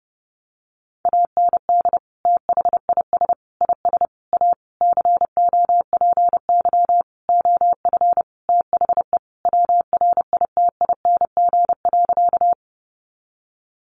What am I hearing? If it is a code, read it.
Morse code, heard as a keyed tone, about 30 wpm, 700 Hz. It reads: AND THIS IS A COPY OF THE WRITING.
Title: Sentences from Top 1400 Words 30wpm